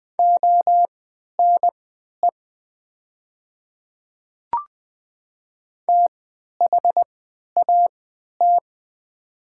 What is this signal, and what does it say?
Morse code, heard as a keyed tone, about 20 wpm, spaced as for 11 wpm, 700 Hz. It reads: ONE THAT
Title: 3x Character Spacing - Top 100 Words - 20wpm